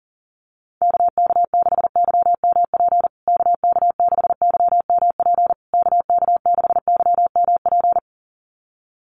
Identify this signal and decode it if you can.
Morse code, heard as a keyed tone, about 40 wpm, 700 Hz. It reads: KK6YMP KK6YMP KK6YMP
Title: USA Call Signs Repeated 3x 40wpm